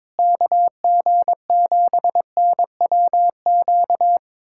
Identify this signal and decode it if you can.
Morse code, heard as a keyed tone, about 21 wpm, 700 Hz. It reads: KG7NWQ
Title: USA Callsign DE Callsign 22wpm